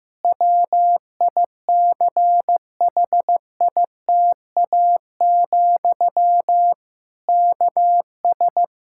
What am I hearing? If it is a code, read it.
Morse code, heard as a keyed tone, about 15 wpm, 700 Hz. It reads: WICHITA, KS
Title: K1USN SST QSO Practice 15wpm